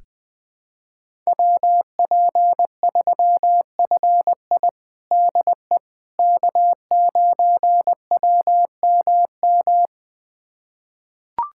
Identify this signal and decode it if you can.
Morse code, heard as a keyed tone, about 20 wpm, 700 Hz. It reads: WP3FI DE K9WMM
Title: USA Callsign DE Callsign - 5 Characters - 20wpm